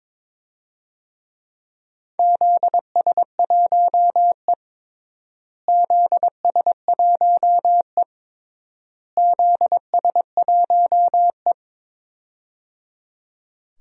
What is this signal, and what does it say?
Morse code, heard as a keyed tone, about 22 wpm, 700 Hz. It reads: ZS1E ZS1E ZS1E
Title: DXCC-100 Call Signs - 4 Characters - Repeated 3x - 22wpm